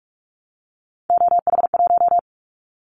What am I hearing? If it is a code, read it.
Morse code, heard as a keyed tone, about 44 wpm, 700 Hz. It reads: OH1
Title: Callsign Prefixes for Top 25 DXCC - 45wpm